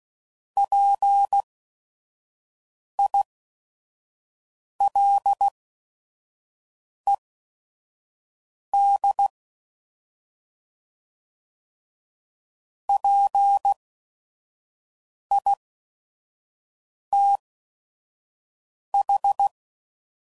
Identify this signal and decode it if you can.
Morse code, heard as a keyed tone, about 16 wpm, spaced as for 5 wpm, 800 Hz. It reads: PILED PITH
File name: Prefixes / By First Letter / Ps